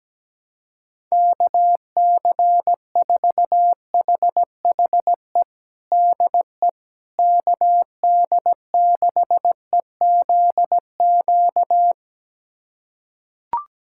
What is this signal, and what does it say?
Morse code, heard as a keyed tone, about 17 wpm, 700 Hz. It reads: KC4HHE DE KD6EZQ
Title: USA Callsign DE Callsign - 6 Characters - 17wpm